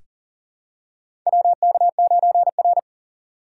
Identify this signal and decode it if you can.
Morse code, heard as a keyed tone, about 40 wpm, 700 Hz. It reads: WK9R
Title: USA Call Signs - 2x Character Spacing - 40wpm